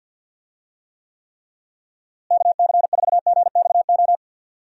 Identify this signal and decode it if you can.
Morse code, heard as a keyed tone, about 53 wpm, 700 Hz. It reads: KC4GXO
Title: USA Call Signs - 6 Characters - Repeated 3x - 50wpm